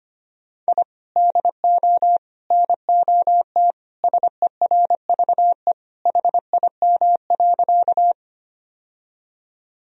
Morse code, heard as a keyed tone, about 25 wpm, 700 Hz. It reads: I DO NOT SERVE HIM.
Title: Sentences from Top 1000 Words - Review Part 2 - 25wpm